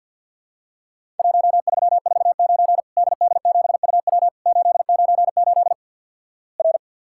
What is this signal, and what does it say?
Morse code, heard as a keyed tone, about 50 wpm, 700 Hz. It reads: 1349 DD7UW 898 R
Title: Worked All Europe DX Contest - QTC Exchange with Repeats - 50wpm